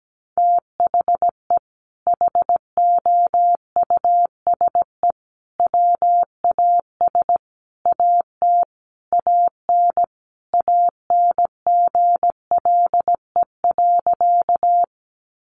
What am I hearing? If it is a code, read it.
Morse code, heard as a keyed tone, about 17 wpm, 700 Hz. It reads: THE HOUSE WAS AT AN ANGLE.
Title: Sentences from Top 1500 Words 17wpm